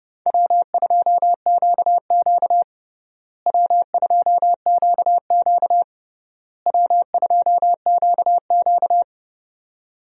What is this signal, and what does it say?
Morse code, heard as a keyed tone, about 30 wpm, 700 Hz. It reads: W2QQ W2QQ W2QQ
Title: USA Call Signs - 4 Characters - Repeated 3x - 30wpm